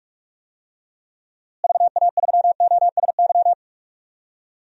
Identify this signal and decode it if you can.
Morse code, heard as a keyed tone, about 46 wpm, 700 Hz. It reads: VA3OSY
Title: DXCC-100 Call Signs - 6 Characters - Repeated 2x - 45wpm